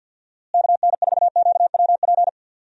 Morse code, heard as a keyed tone, about 52 wpm, 700 Hz. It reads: KN4QWP
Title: USA Call Signs - 6 Characters - Repeated 3x - 50wpm